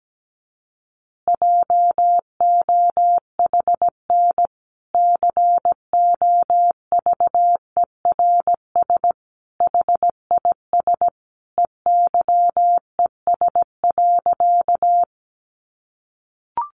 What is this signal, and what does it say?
Morse code, heard as a keyed tone, about 17 wpm, 700 Hz. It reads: JOHN COVERS HIS EYES.
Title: Easy Phrases 17wpm